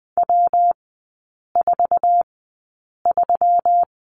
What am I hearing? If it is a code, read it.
Morse code, heard as a keyed tone, about 20 wpm, spaced as for 8 wpm, 700 Hz. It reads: W43
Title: 121 Lesson 20e - Number 3 - Three Characters 20wpm